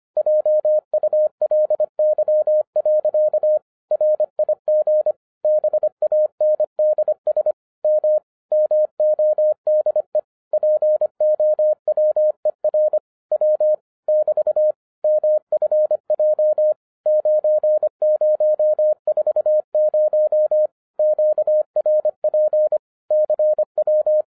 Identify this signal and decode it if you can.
Morse code, heard as a keyed tone, about 25 wpm, 610 Hz. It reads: JULY. RIG BANDS M MODE POWER W <BT> MFJ 9040 QRP CW
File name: 2011-07-mp3 / n24 QRP OTAKU 25wpm